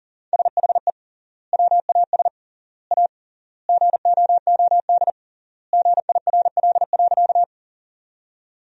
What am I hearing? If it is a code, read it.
Morse code, heard as a keyed tone, about 39 wpm, 700 Hz. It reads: SHE WAS A GOOD GIRL.
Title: 3x Word-Spacing - Sentences from Top 400 Words - 40wpm